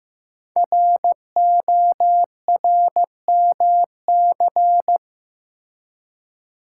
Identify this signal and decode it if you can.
Morse code, heard as a keyed tone, about 15 wpm, 700 Hz. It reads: RORMC
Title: Code Groups - 5 Letters - 15wpm